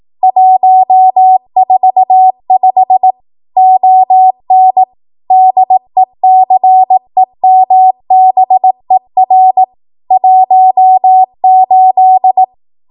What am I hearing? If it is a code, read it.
Morse code, heard as a keyed tone, about 18 wpm, 750 Hz. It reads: 145 ON DECEMBER 18
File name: Archive / 18 WPM / 18WPM